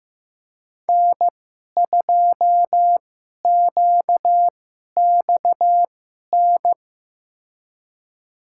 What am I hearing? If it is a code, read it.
Morse code, heard as a keyed tone, about 15 wpm, spaced as for 11 wpm, 700 Hz. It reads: N2QXN
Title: USA Call Signs - 5 Characters - 2x Character Spacing - 15wpm